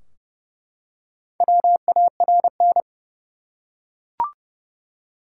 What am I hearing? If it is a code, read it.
Morse code, heard as a keyed tone, about 30 wpm, 700 Hz. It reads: WARN
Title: Four Letter Words - 30wpm